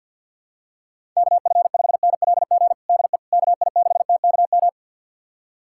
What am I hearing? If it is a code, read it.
Morse code, heard as a keyed tone, about 53 wpm, 700 Hz. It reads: KF5NLG DE KI6TKM
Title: USA Callsign DE Callsign - 6 Characters - 50wpm